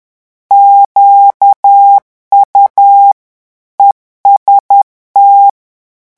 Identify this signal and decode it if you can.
Morse code, heard as a keyed tone, about 10 wpm, 800 Hz. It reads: QU EST
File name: revision9 citations humoristiques vitesse 10 mots minute